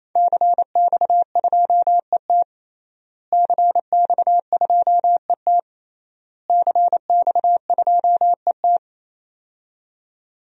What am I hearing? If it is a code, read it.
Morse code, heard as a keyed tone, about 27 wpm, 700 Hz. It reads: CX2ET CX2ET CX2ET
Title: DXCC-100 Call Signs - 5 Characters - Repeated 3x - 28wpm